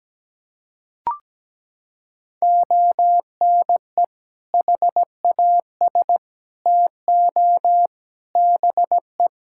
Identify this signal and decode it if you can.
Morse code, heard as a keyed tone, about 17 wpm, 700 Hz. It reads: ONE HAS TO BE
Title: Sentences from Top 1600 Words 17wpm